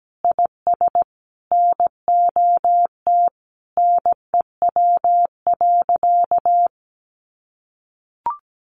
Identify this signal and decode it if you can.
Morse code, heard as a keyed tone, about 17 wpm, 700 Hz. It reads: IS NOT NEW.
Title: Sentences from Top 1100 Words 17wpm